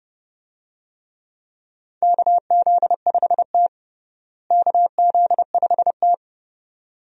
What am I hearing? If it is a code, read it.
Morse code, heard as a keyed tone, about 30 wpm, 700 Hz. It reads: KZ5T KZ5T
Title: USA Call Signs - 4 Characters - Repeated 2x - 30wpm